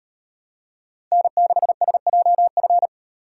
Morse code, heard as a keyed tone, about 32 wpm, 700 Hz. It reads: N6SJF